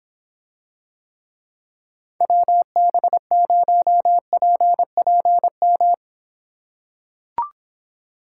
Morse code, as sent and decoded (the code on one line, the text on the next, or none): .-- -... ----- .--. .--. --
WB0PPM